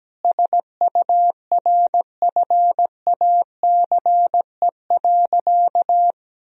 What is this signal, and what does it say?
Morse code, heard as a keyed tone, about 17 wpm, 700 Hz. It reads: SURFACE.
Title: Sentences from Top 1000 Words - Review Part 3 - 17wpm